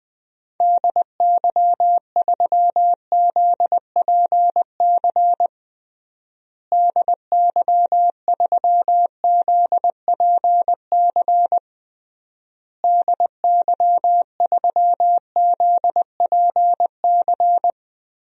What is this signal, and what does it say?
Morse code, heard as a keyed tone, about 20 wpm, 700 Hz. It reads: DY3ZPC DY3ZPC DY3ZPC
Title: DXCC-100 Call Signs - 6 Characters - Repeated 3x - 20wpm